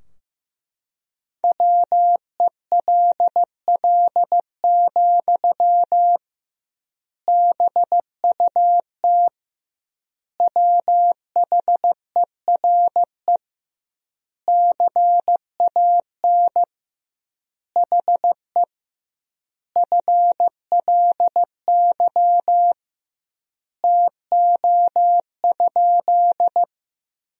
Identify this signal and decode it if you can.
Morse code, heard as a keyed tone, about 15 wpm, 700 Hz. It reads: WELL, BUT WHERE CAN HE FLY TO?
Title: 2x Word-Spacing - Sentences from Top 500 Words - 15wpm